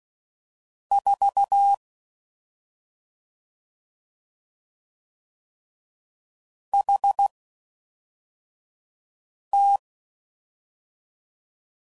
Morse code, heard as a keyed tone, about 16 wpm, spaced as for 4 wpm, 800 Hz. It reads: 4 HT
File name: Mixed / CW Comparisons / CW Comparisons Broken Characters